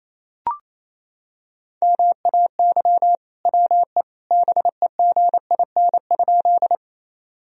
Morse code, heard as a keyed tone, about 28 wpm, 700 Hz. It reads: MAY WE BEGIN?
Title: Sentences from Top 1000 Words - Review Part 1 - 28wpm